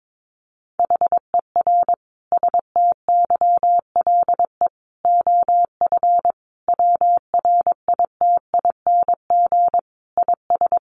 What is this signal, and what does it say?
Morse code, heard as a keyed tone, about 22 wpm, 700 Hz. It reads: HER STYLE OF WRITING IS
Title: Sentences from Top 800 Words 22wpm